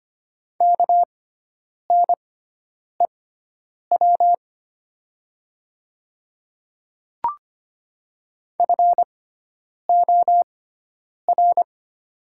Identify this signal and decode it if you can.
Morse code, heard as a keyed tone, about 25 wpm, spaced as for 9 wpm, 700 Hz. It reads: KNEW FOR
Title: 6x Character Spacing - Top 500 Words - 25wpm